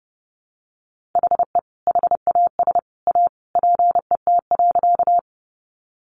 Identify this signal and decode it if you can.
Morse code, heard as a keyed tone, about 30 wpm, 700 Hz. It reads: HE HAS A PET.
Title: Easy Phrases 30wpm